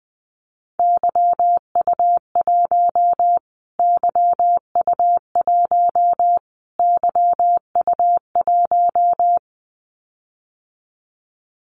Morse code, heard as a keyed tone, about 19 wpm, 700 Hz. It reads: YU1 YU1 YU1
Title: Callsign Prefixes for Top 25 DXCC - Repeated 3x - 20wpm